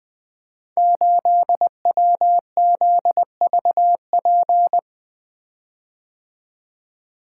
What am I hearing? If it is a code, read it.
Morse code, heard as a keyed tone, about 20 wpm, 700 Hz. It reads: 8WZVP
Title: Code Groups - 5 Letter:Numbers - 20wpm